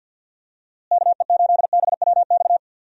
Morse code, heard as a keyed tone, about 51 wpm, 700 Hz. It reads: KE8BWX
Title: USA Call Signs - 6 Characters - Repeated 3x - 50wpm